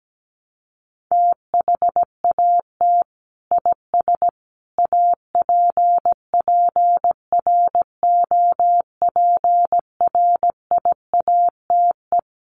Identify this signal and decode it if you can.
Morse code, heard as a keyed tone, about 17 wpm, 700 Hz. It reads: THAT IS APPROPRIATE
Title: Sentences from Top 900 Words 17wpm